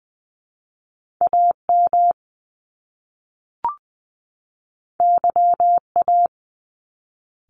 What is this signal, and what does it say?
Morse code, heard as a keyed tone, about 20 wpm, 700 Hz. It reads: A M Y A
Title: Top Two Letter Combinations in Words 20wpm